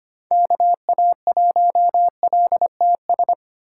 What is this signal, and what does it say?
Morse code, heard as a keyed tone, about 25 wpm, 700 Hz. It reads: KA1LTS
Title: USA Call Signs - 2x Character Spacing - 25wpm